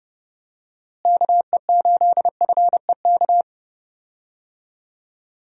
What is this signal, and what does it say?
Morse code, heard as a keyed tone, about 32 wpm, 700 Hz. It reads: KE8FEK
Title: USA Call Signs - 6 Characters - Repeated 3x - 30wpm